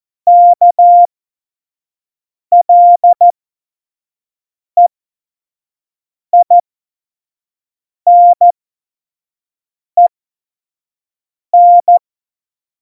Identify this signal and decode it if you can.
Morse code, heard as a keyed tone, about 14 wpm, spaced as for 5 wpm, 700 Hz. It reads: KLEINEN
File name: Morsekurs / Witze / Witz 15. September 2020, 25 F70